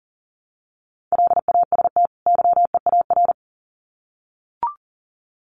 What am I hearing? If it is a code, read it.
Morse code, heard as a keyed tone, about 40 wpm, 700 Hz. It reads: LAST YEAR
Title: Sets of 2 Words 40wpm